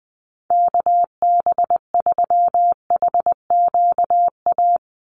Morse code, heard as a keyed tone, about 20 wpm, 700 Hz. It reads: KB3HQA
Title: USA Call Signs 20wpm